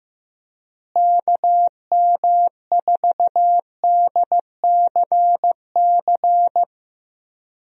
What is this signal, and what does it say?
Morse code, heard as a keyed tone, about 15 wpm, 700 Hz. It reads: KM4DCC
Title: USA Call Signs - 6 Characters - Repeated 3x - 15wpm